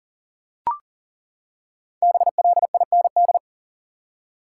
Morse code, heard as a keyed tone, about 40 wpm, 700 Hz. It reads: BLIND
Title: Five Letter Words - 40wpm